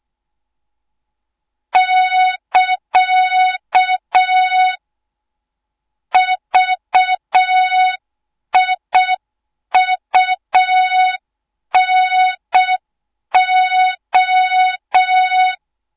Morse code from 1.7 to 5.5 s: <KA>